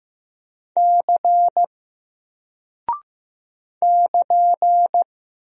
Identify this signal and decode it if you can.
Morse code, heard as a keyed tone, about 15 wpm, 700 Hz. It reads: C <KN>
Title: Single Character (Letter:Number:Punctuation:Prosign) Alt 15wpm